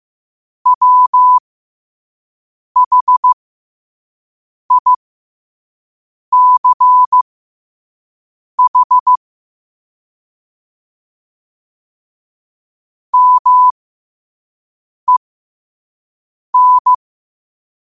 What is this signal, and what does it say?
Morse code, heard as a keyed tone, about 15 wpm, spaced as for 5 wpm, 1000 Hz. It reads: WHICH MEN
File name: qotdfiles / 5wpm